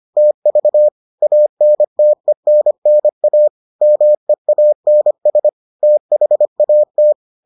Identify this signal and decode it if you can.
Morse code, heard as a keyed tone, about 25 wpm, 600 Hz. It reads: TV ANTENNA MEANS THAT